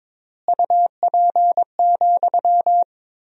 Morse code, heard as a keyed tone, about 22 wpm, 700 Hz. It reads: UP,
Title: 4x Word-Spacing - Sentences from Top 500 Words - 22wpm